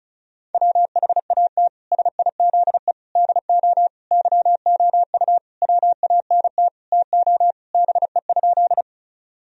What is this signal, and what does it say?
Morse code, heard as a keyed tone, about 34 wpm, 700 Hz. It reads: WHAT SIZE DO YOU WANT TO BE?
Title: Sentences from Top 1000 Words - Review Part 2 - 35wpm